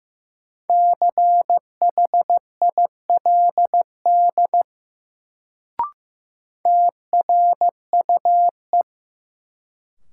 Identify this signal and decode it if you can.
Morse code, heard as a keyed tone, about 15 wpm, 700 Hz. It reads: CHILD TRUE
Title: Sets of 5 Words Encore - New Word Drill - 15wpm